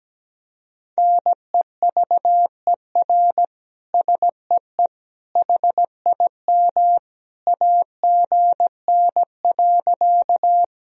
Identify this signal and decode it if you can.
Morse code, heard as a keyed tone, about 17 wpm, 700 Hz. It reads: NEVER SEE HIM AGN.
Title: Sentences from Top 300 Words and Abbreviations 17wpm